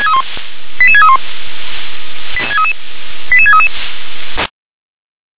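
Five-tone selective call (ZVEI), heard as quick runs of five tones, tones of about 70 ms. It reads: 88631 88633
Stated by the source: ZVEI five-tone selective call received on SDR.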